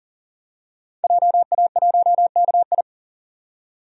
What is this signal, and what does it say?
Morse code, heard as a keyed tone, about 39 wpm, 700 Hz. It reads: JA1KI
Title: DXCC-100 Call Signs - 5 Characters - 40wpm